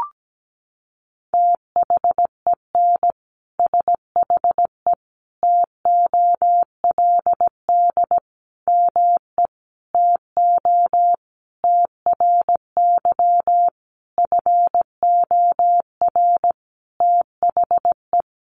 Morse code, heard as a keyed tone, about 17 wpm, 700 Hz. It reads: THEN SHE TOLD ME TO TRY FOR THE
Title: Sentences from Top 400 Words 17wpm